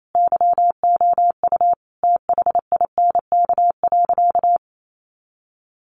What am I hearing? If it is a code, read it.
Morse code, heard as a keyed tone, about 28 wpm, 700 Hz. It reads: YOU THINK.
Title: Sentences from Top 600 Words 28wpm